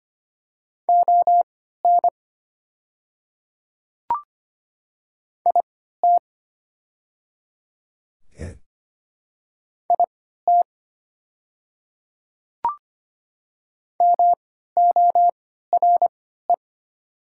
Morse code, heard as a keyed tone, about 25 wpm, spaced as for 14 wpm, 700 Hz. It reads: ON IT IT MORE
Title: 3x Character Spacing - Top 100 Words - 25wpm